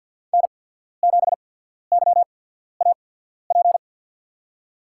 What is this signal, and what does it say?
Morse code, heard as a keyed tone, about 51 wpm, spaced as for 14 wpm, 700 Hz. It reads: N7YAP
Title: USA Call Signs - 5 Characters - 8x Character Spacing - 50wpm